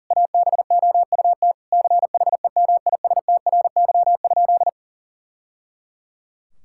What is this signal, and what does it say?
Morse code, heard as a keyed tone, about 37 wpm, 700 Hz. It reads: ABOUT CHEMISTRY?
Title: Sentences from Top 1000 Words - Review Part 6 - 40wpm